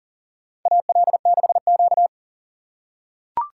Morse code, heard as a keyed tone, about 40 wpm, 700 Hz. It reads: AL6Q
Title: USA Call Signs - 4 Characters - 3x Character Spacing - 40wpm